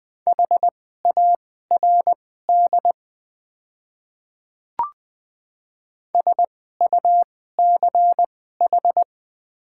Morse code, heard as a keyed tone, about 21 wpm, spaced as for 15 wpm, 700 Hz. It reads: HARD SUCH
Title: 2x Character Spacing - Top 300 Words - 20wpm